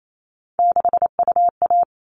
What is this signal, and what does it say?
Morse code, heard as a keyed tone, about 28 wpm, 700 Hz. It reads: 6UA
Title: All Cut Numbers - 3-Digit 28wpm